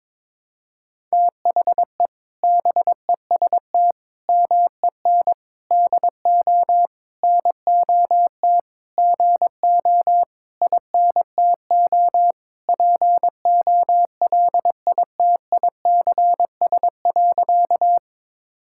Morse code, heard as a keyed tone, about 22 wpm, 700 Hz. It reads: THE BEST MEN DO NOT GO INTO POLITICS.
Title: Sentences from Top 1000 Words - Review Part 6 - 22wpm